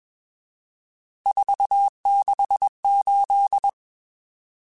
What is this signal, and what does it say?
Morse code, heard as a keyed tone, about 21 wpm, 800 Hz. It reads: 468
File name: RSTs / RST-mp3s / Various RST @20